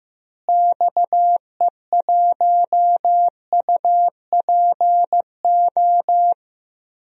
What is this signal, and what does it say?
Morse code, heard as a keyed tone, about 15 wpm, 700 Hz. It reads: XE1UPO
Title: DXCC-100 Call Signs - 6 Characters - Repeated 3x - 15wpm